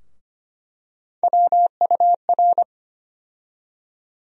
Morse code, heard as a keyed tone, about 25 wpm, 700 Hz. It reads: WUR
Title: USA Call Sign Postfixes - 3 Letters 25wpm